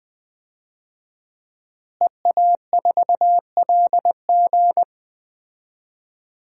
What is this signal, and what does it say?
Morse code, heard as a keyed tone, about 20 wpm, 700 Hz. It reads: EA4LG
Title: DXCC-100 Call Signs - 5 Characters - Repeated 3x - 20wpm